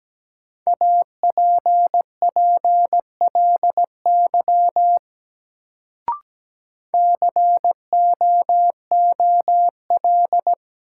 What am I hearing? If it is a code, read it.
Morse code, heard as a keyed tone, about 17 wpm, 700 Hz. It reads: APPLY COOL
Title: Sentences from Top 1200 Words - New Word Drill - 17wpm